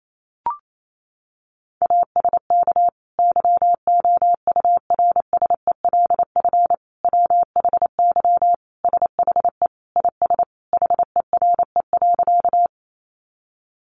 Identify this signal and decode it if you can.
Morse code, heard as a keyed tone, about 28 wpm, 700 Hz. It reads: ASK YOURSELF WHY SHE IS HERE.
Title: Sentences from Top 600 Words 28wpm